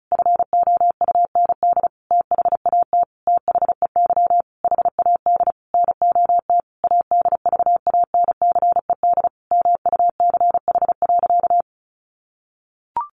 Morse code, heard as a keyed tone, about 34 wpm, 700 Hz. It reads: FOUND THAT THEY HAD NOT ADVANCED MUCH.
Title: Sentences from Top 1200 Words 35wpm